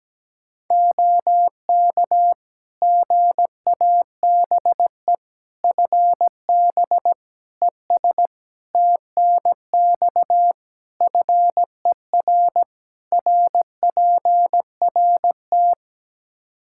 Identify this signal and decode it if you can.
Morse code, heard as a keyed tone, about 17 wpm, 700 Hz. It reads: OK GABE FB ES TNX FER RPRT
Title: Practice QSO - Exchange 2 - Info - 17wpm